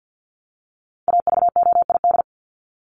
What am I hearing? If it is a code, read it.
Morse code, heard as a keyed tone, about 50 wpm, 700 Hz. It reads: AVOID
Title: Top 751-1000 Words 50wpm